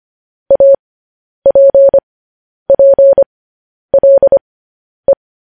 Morse code, heard as a keyed tone, about 25 wpm, spaced as for 10 wpm, 550 Hz. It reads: APPLE